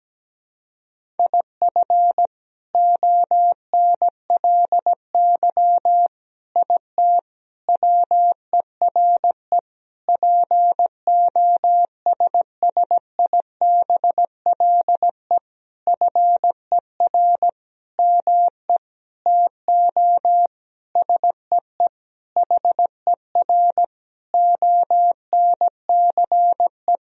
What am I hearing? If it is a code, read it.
Morse code, heard as a keyed tone, about 17 wpm, 700 Hz. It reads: IF ONLY IT WERE POSSIBLE FER ME TO SEE HER ONCE
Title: Sentences from Top 500 Words and Abbreviations 17wpm